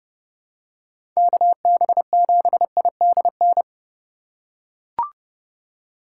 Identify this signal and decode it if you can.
Morse code, heard as a keyed tone, about 30 wpm, 700 Hz. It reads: KB7IDN
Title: USA Call Signs - 6 Characters - 3x Character Spacing - 30wpm